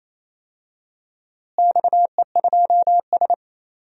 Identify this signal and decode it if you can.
Morse code, heard as a keyed tone, about 28 wpm, 700 Hz. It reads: XE2S
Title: DXCC-100 Call Signs - 4 Characters - Repeated 3x - 28wpm